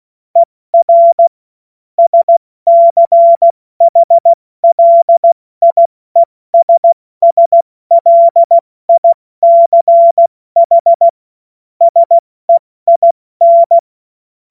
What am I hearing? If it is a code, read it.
Morse code, heard as a keyed tone, about 16 wpm, 680 Hz. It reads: ER SCHLIESSLICH SEIN
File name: Morsekurs / Witze / Witz 21. Dezember 2021, 70 F80